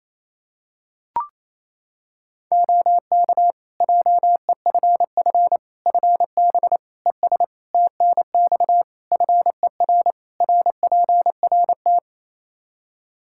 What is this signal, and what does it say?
Morse code, heard as a keyed tone, about 27 wpm, 700 Hz. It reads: OK JEFF FB ES TNX FER RPRT
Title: K1USN SST QSO Practice 28wpm